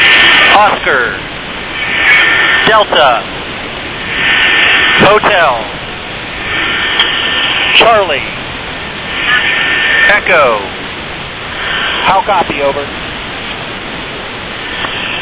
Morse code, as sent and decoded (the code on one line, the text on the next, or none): none